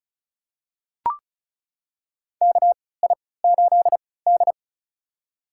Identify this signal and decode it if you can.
Morse code, heard as a keyed tone, about 34 wpm, spaced as for 20 wpm, 700 Hz. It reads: KI8D
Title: USA Call Signs - 4 Characters - 3x Character Spacing - 35wpm